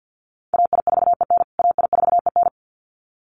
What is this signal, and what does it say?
Morse code, heard as a keyed tone, about 50 wpm, 700 Hz. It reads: AI4EN AI4EN
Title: USA Call Signs Repeated 2x 50wpm